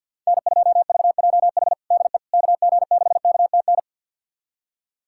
Morse code, heard as a keyed tone, about 50 wpm, 700 Hz. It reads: N2VJH DE KG6KTN
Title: USA Callsign DE Callsign 50wpm